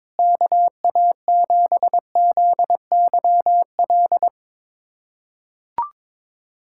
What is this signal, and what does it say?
Morse code, heard as a keyed tone, about 22 wpm, 700 Hz. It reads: KA7ZYL
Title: USA Callsign DE Callsign - 6 Characters - 22wpm